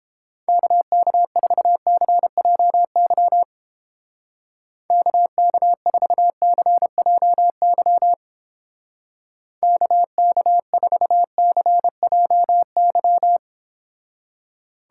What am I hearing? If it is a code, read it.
Morse code, heard as a keyed tone, about 29 wpm, 700 Hz. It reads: KK4CJY KK4CJY KK4CJY